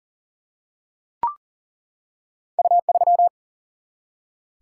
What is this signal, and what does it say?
Morse code, heard as a keyed tone, about 40 wpm, 700 Hz. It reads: U3
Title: Single-Letter Single-Number 40wpm